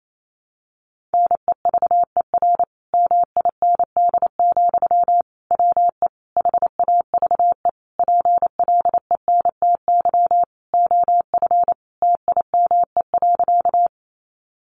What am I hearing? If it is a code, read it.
Morse code, heard as a keyed tone, about 28 wpm, 700 Hz. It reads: NEVER MIND, WE HAVE PLENTY OF TIME.
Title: Sentences from Top 1300 Words 28wpm